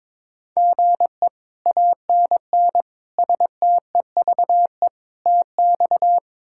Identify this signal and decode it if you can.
Morse code, heard as a keyed tone, about 22 wpm, 700 Hz. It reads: GE ANN STEVE TX
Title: K1USN SST QSO Practice - 22wpm